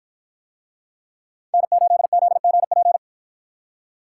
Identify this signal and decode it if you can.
Morse code, heard as a keyed tone, about 53 wpm, 700 Hz. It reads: N8ZGP